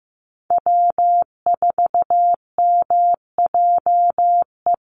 Morse code, heard as a keyed tone, about 15 wpm, 700 Hz. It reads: W4MJE
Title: Call Signs Repeated 3x - 15wpm